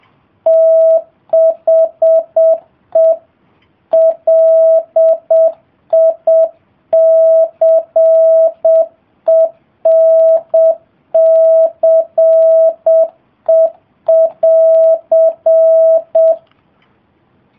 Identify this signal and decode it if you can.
Morse code, heard as a keyed tone, about 7 wpm, 660 Hz. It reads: THE LICENCE<AR>